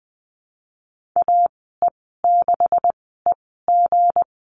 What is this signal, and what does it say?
Morse code, heard as a keyed tone, about 20 wpm, spaced as for 14 wpm, 700 Hz. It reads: AE6EG
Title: 238 Lesson 38g - Letter q - Call Signs 20wpm